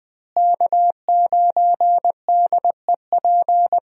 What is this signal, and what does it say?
Morse code, heard as a keyed tone, about 20 wpm, 700 Hz. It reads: K9DEP
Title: USA Call Signs - 5 Characters - Repeated 3x - 20wpm